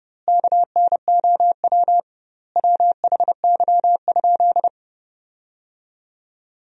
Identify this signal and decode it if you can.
Morse code, heard as a keyed tone, about 30 wpm, 700 Hz. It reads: KNOW WHY?
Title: 2x Word-Spacing - Sentences from Top 200 Words - 30wpm